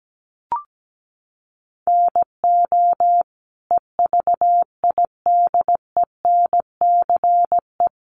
Morse code, heard as a keyed tone, about 17 wpm, 700 Hz. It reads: NO EVIDENCE
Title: Sentences from Top 1300 Words 17wpm